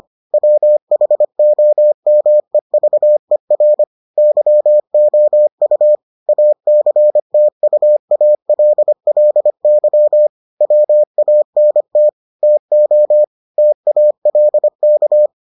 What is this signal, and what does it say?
Morse code, heard as a keyed tone, about 25 wpm, 600 Hz. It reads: WHOMEVER YOU ACTUALLY WANT TO TALK